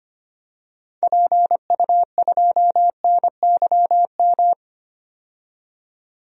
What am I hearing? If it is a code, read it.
Morse code, heard as a keyed tone, about 25 wpm, 700 Hz. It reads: PU2NYM
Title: DXCC-100 Call Signs - 6 Characters - 25wpm